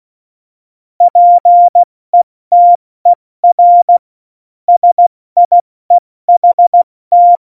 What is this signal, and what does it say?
Morse code, heard as a keyed tone, about 16 wpm, 700 Hz. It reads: PETER SIEHT
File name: Morsekurs / Witze / Witz 29. August 17, Tempo 70 F80